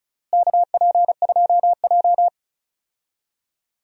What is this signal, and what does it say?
Morse code, heard as a keyed tone, about 35 wpm, 700 Hz. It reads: KP2J
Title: DXCC-100 Call Signs - 4 Characters - Repeated 2x - 35wpm